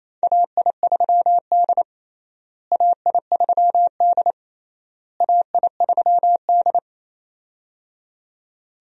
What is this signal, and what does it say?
Morse code, heard as a keyed tone, about 27 wpm, 700 Hz. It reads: AI3D AI3D AI3D
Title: USA Call Signs - 4 Characters - Repeated 3x - 28wpm